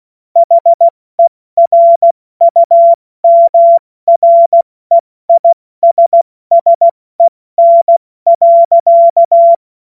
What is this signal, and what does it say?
Morse code, heard as a keyed tone, about 16 wpm, 680 Hz. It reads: HERUMREISSEN.
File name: Morsekurs / Witze / Witz 31. Mai 2022 70 F80